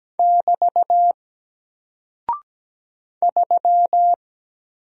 Morse code, heard as a keyed tone, about 17 wpm, 700 Hz. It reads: <BT> 3
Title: Single Character (Letter:Number:Punctuation:Prosign) Alt 17wpm